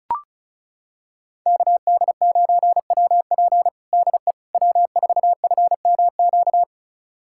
Morse code, heard as a keyed tone, about 36 wpm, 700 Hz. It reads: KD9WP DE W4FMQ
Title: USA Callsign DE Callsign - 5 Characters - 35wpm